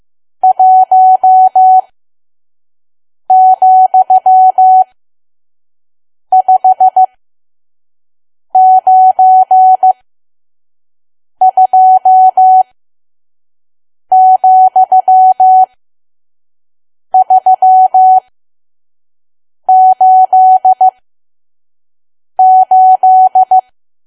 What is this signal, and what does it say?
Morse code, heard as a keyed tone, about 15 wpm, spaced as for 5 wpm, 750 Hz. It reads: 1,592,388
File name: Archive / 5 WPM files / PM